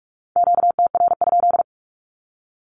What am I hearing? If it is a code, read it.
Morse code, heard as a keyed tone, about 44 wpm, 700 Hz. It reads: QTR?
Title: Q-Codes 45wpm